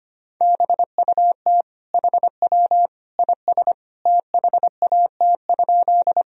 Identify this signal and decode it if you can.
Morse code, heard as a keyed tone, about 25 wpm, 700 Hz. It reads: BUT HW IS THAT?
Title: Sentences from Top 100 Words and Abbreviations 25wpm